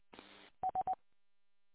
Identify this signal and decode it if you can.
There is no code.